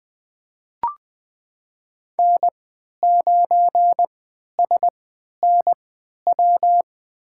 Morse code, heard as a keyed tone, about 20 wpm, spaced as for 11 wpm, 700 Hz. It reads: N9SNW
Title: USA Call Signs - 5 Characters - 3x Character Spacing - 20wpm